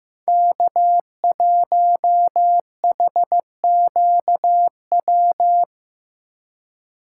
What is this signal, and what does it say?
Morse code, heard as a keyed tone, about 15 wpm, 700 Hz. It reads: K1HQW
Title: USA Call Signs - 5 Characters - 6x Character Spacing - 15wpm